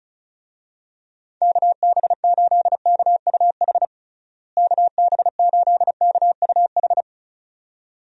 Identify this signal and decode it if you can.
Morse code, heard as a keyed tone, about 34 wpm, 700 Hz. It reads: KB8KUH KB8KUH
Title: USA Call Signs - 6 Characters - Repeated 2x - 35wpm